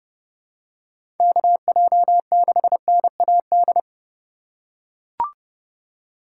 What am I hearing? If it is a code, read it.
Morse code, heard as a keyed tone, about 30 wpm, 700 Hz. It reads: KJ6NAD
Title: USA Call Signs - 6 Characters - 2x Character Spacing - 30wpm